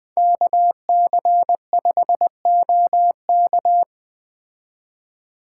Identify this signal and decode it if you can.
Morse code, heard as a keyed tone, about 20 wpm, 700 Hz. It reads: KC5OK
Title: Speed Racing - USA 2x2 Call Signs 17wpm